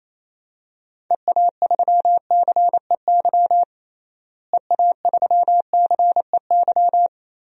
Morse code, heard as a keyed tone, about 28 wpm, 700 Hz. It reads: EA3CEY EA3CEY
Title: DXCC-100 Call Signs - 6 Characters - Repeated 2x - 28wpm